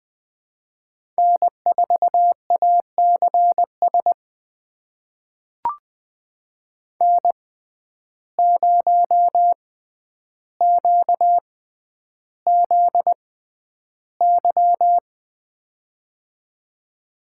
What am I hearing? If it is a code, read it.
Morse code, heard as a keyed tone, about 20 wpm, spaced as for 10 wpm, 700 Hz. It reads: N4ACS N0QZY
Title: USA Call Signs - 5 Characters - 6x Character Spacing - 20wpm